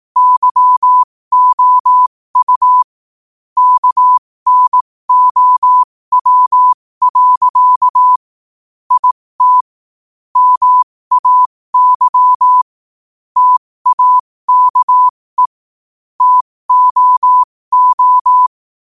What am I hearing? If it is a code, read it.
Morse code, heard as a keyed tone, about 18 wpm, spaced as for 15 wpm, 1000 Hz. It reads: YOU KNOW. IT MAY TAKE TOO
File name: qotdfiles / 15wpm